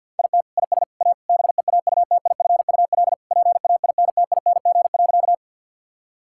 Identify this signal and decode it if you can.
Morse code, heard as a keyed tone, about 50 wpm, 700 Hz. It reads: IT IS A BEAUTIFUL PAINTING.
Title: Sentences from Top 1000 Words - Review Part 6 - 50wpm